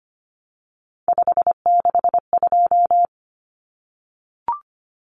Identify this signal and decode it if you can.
Morse code, heard as a keyed tone, about 25 wpm, 700 Hz. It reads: 562
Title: Numbers - 3-Digit 25wpm